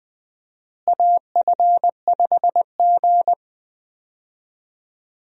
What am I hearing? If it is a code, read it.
Morse code, heard as a keyed tone, about 21 wpm, 700 Hz. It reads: AF5G
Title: USA Call Signs - 4 Characters - Repeated 2x - 20wpm